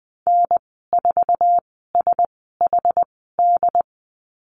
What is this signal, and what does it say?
Morse code, heard as a keyed tone, about 20 wpm, spaced as for 14 wpm, 700 Hz. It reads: N4SHD
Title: 076 Lesson 13g - Letter L - Call Signs 20wpm